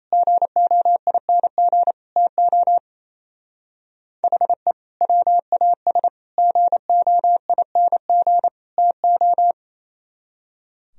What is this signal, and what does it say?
Morse code, heard as a keyed tone, about 34 wpm, 700 Hz. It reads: GOING TO HE WAS GOING TO
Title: Speed-Racing - Sets of 4 Words 28wpm